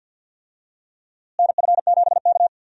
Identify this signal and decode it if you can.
Morse code, heard as a keyed tone, about 46 wpm, 700 Hz. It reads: NF7K
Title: USA Call Signs - 4 Characters - Repeated 2x - 50wpm